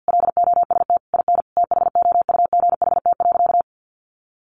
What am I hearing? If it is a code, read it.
Morse code, heard as a keyed tone, about 50 wpm, 700 Hz. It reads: LOST IN THOUGHT.
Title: Sentences from Top 1400 Words 50wpm